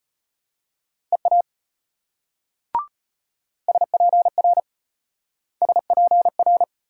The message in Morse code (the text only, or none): EA SPR SPR